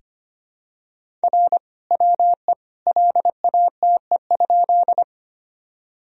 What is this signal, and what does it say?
Morse code, heard as a keyed tone, about 25 wpm, 700 Hz. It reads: R WE LATE?
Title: Sentences from Top 300 Words and Abbreviations 25wpm